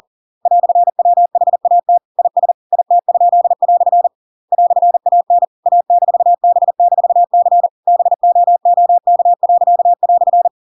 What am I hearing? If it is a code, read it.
Morse code, heard as a keyed tone, about 40 wpm, 700 Hz. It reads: "WHAT IS IT?" "AN A-B-C BOOK."
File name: Pinocchio / Pinocchio ch8 40 40 ik1hge